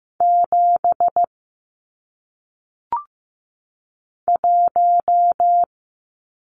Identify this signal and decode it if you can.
Morse code, heard as a keyed tone, about 15 wpm, 700 Hz. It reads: 7 1
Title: Numbers - 1-Digit 15wpm